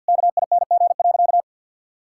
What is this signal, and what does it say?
Morse code, heard as a keyed tone, about 52 wpm, 700 Hz. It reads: KING.
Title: Sentences from Top 1000 Words - Review Part 6 - 50wpm